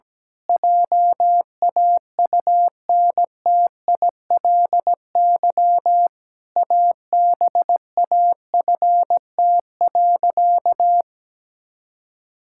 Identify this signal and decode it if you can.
Morse code, heard as a keyed tone, about 17 wpm, 700 Hz. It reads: JAUNTILY ABAFT.
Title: Pangram Challenge v2 - 17wpm